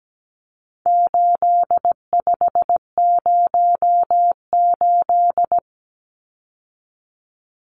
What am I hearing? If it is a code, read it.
Morse code, heard as a keyed tone, about 17 wpm, 700 Hz. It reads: 8508
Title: Numbers - 4-Digit 17wpm